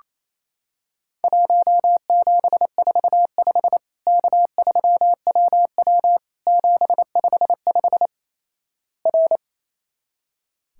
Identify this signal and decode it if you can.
Morse code, heard as a keyed tone, about 28 wpm, 700 Hz. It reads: 1745 K3WW 755 R
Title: Worked All Europe DX Contest - QTC Exchange - 28wpm